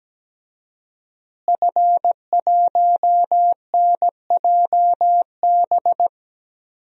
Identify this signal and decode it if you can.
Morse code, heard as a keyed tone, about 17 wpm, 700 Hz. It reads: F1NJB
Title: DXCC-100 Call Signs - 5 Characters - Repeated 2x - 17wpm